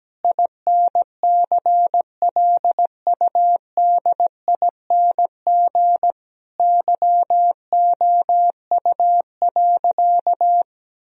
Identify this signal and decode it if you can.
Morse code, heard as a keyed tone, about 17 wpm, 700 Hz. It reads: INCLUDING YOU.
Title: Sentences from Top 1000 Words - Review Part 4 - 17wpm